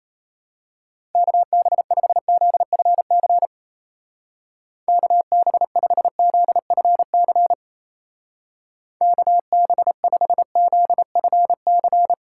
Morse code, heard as a keyed tone, about 32 wpm, 700 Hz. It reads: KB5ZFC KB5ZFC KB5ZFC
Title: Speed Racing - USA 2x3 Call Signs 28wpm